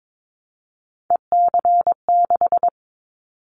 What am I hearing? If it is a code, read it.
Morse code, heard as a keyed tone, about 22 wpm, 700 Hz. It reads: EC6
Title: Callsign Prefixes for Top 50 DXCC - 22wpm